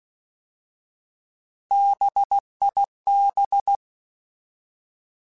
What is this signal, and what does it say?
Morse code, heard as a keyed tone, about 16 wpm, 800 Hz. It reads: BIB
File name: Alphabetical / Alphabetical mp3s / Bs @15